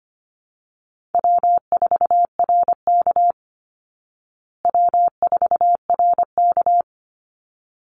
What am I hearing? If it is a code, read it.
Morse code, heard as a keyed tone, about 25 wpm, 700 Hz. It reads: W4RK W4RK
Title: Easy Call Signs Repeated 2x - 25wpm